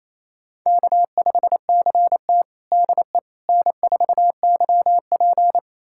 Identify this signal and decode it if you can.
Morse code, heard as a keyed tone, about 28 wpm, 700 Hz. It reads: K5CT DE N4YP
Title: USA Callsign DE Callsign - 4 Characters - 28wpm